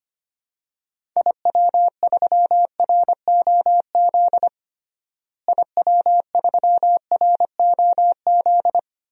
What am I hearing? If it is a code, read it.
Morse code, heard as a keyed tone, about 25 wpm, 700 Hz. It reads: IW3ROZ IW3ROZ
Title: DXCC-100 Call Signs - 6 Characters - Repeated 2x - 25wpm